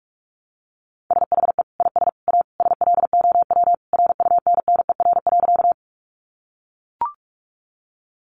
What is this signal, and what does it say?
Morse code, heard as a keyed tone, about 44 wpm, 700 Hz. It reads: SHE IS A SLOW RUNNER.
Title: Additional Easy Phrases 45wpm